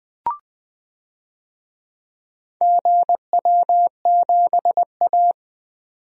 Morse code, heard as a keyed tone, about 20 wpm, 700 Hz. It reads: GW7A
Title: DXCC-100 Call Signs - 4 Characters - 20wpm